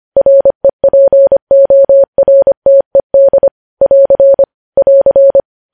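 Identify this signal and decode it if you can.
Morse code, heard as a keyed tone, about 25 wpm, 550 Hz. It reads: REPORTED <AR> <AR>
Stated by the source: Morse code news headlines sent at 25 wpm.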